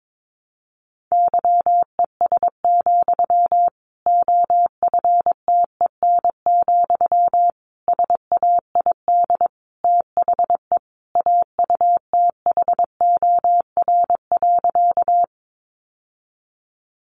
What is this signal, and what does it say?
Morse code, heard as a keyed tone, about 22 wpm, 700 Hz. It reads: YES, OFTEN, SAID THE AUTHOR.
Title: Sentences from Top 1200 Words 22wpm